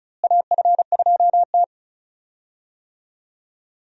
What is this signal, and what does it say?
Morse code, heard as a keyed tone, about 34 wpm, 700 Hz. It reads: AF2T
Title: USA Call Signs - 4 Characters - Repeated 3x - 35wpm